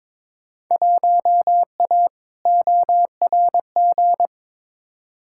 Morse code, heard as a keyed tone, about 22 wpm, 700 Hz. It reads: 1A ORG
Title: Field Day Exchanges - 22wpm